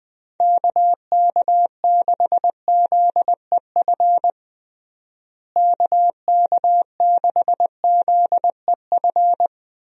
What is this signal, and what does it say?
Morse code, heard as a keyed tone, about 20 wpm, 700 Hz. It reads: KK6ZEF KK6ZEF
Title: USA Call Signs - 6 Characters - Repeated 3x - 20wpm